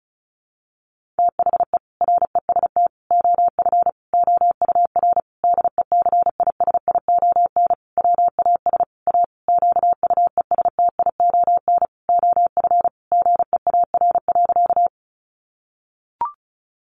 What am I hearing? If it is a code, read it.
Morse code, heard as a keyed tone, about 34 wpm, 700 Hz. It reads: THE REST OF OUR DECISION WAS A QUESTION OF GEAR.
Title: Sentences from Top 1400 Words 35wpm